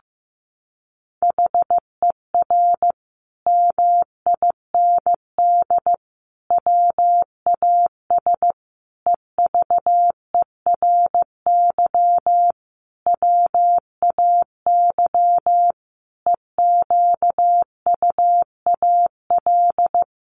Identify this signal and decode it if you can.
Morse code, heard as a keyed tone, about 15 wpm, 700 Hz. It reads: HER MIND WAS EVERY WAY EQUAL
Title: Sentences from Top 1400 Words 15wpm